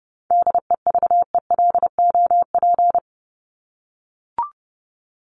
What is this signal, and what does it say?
Morse code, heard as a keyed tone, about 30 wpm, 700 Hz. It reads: DEVELOP
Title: ED Suffix - Top 100 Words with Base Words 30wpm